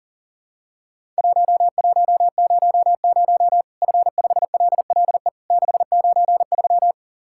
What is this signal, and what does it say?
Morse code, heard as a keyed tone, about 40 wpm, 700 Hz. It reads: 1100 F5LLE 693
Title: Worked All Europe DX Contest - QTC Exchange - 40wpm